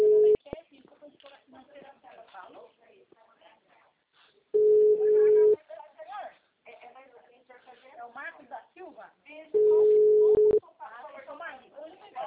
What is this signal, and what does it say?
Ringback tone (Europe, 425 Hz), with speech.
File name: out- -2036- - - .52591